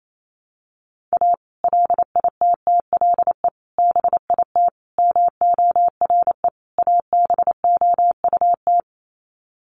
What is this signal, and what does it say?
Morse code, heard as a keyed tone, about 28 wpm, 700 Hz. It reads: A LITTLE BIT MORE ABOUT
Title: Sets of 5 Words Encore2 28wpm